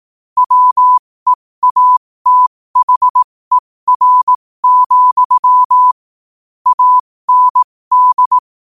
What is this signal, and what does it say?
Morse code, heard as a keyed tone, about 18 wpm, spaced as for 15 wpm, 1000 Hz. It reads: WEATHER, AND
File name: qotdfiles / 15wpm